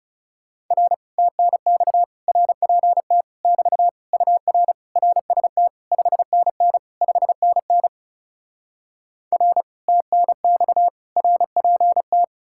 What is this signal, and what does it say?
Morse code, heard as a keyed tone, about 37 wpm, 700 Hz. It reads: R TNX RPT <BT> UR RST 5NN 5NN R TNX RPT
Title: Speed-Racing Common QSO Phrases - 30wpm